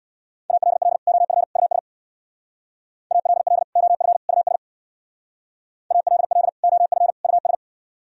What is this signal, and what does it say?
Morse code, heard as a keyed tone, about 62 wpm, 700 Hz. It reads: ALL OF US ALL OF US ALL OF US
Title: Speed-Racing - Sets of 3 Words 50wpm